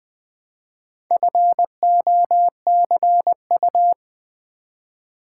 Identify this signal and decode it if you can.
Morse code, heard as a keyed tone, about 20 wpm, 700 Hz. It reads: FOCU
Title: Build A Word - 1 Letter At A Time - 5-Letter Words - 20wpm